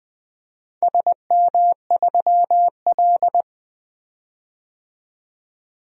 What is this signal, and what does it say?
Morse code, heard as a keyed tone, about 20 wpm, 700 Hz. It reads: SM3L
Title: DXCC-100 Call Signs - 4 Characters - 20wpm